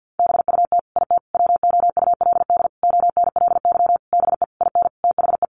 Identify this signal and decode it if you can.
Morse code, heard as a keyed tone, about 50 wpm, 700 Hz. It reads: BUT IT WOULD ONLY BE IN THE
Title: Sentences from Top 1500 Words 50wpm